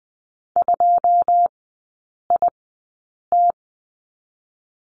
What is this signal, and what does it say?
Morse code, heard as a keyed tone, about 19 wpm, 700 Hz. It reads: 2 I T
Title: 107 Lesson 18e - Letter m - Three Characters 20wpm